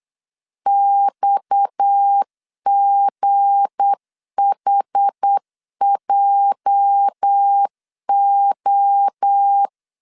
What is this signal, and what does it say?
Morse code, heard as a keyed tone, about 8 wpm, 800 Hz. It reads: XGHJO
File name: LEZIONI / 8 YJ45 / 8 YJ4 8wpm 3246